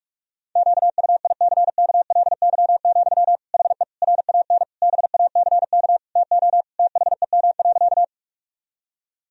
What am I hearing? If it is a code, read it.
Morse code, heard as a keyed tone, about 42 wpm, 700 Hz. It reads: QUICKLY, HE RAN BACK TO THEM.
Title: Sentences from Top 1000 Words - Review Part 3 - 45wpm